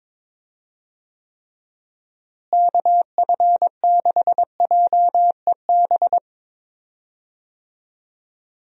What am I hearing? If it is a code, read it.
Morse code, heard as a keyed tone, about 22 wpm, 700 Hz. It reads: KF6JEB
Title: USA Call Signs - 6 Characters - Repeated 2x - 22wpm